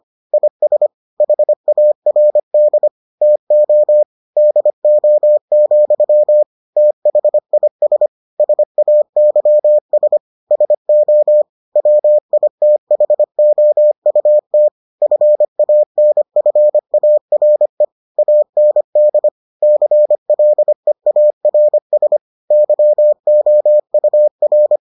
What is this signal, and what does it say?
Morse code, heard as a keyed tone, about 25 wpm, 600 Hz. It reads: IS HARD TO DO, THIS SAYS SO WITHOUT FANFARE AND CLEARS YOUR